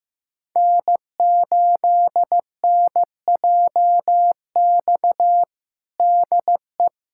Morse code, heard as a keyed tone, about 16 wpm, 700 Hz. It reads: N8NJX DE
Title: USA Callsign DE Callsign - 5 Characters - 15wpm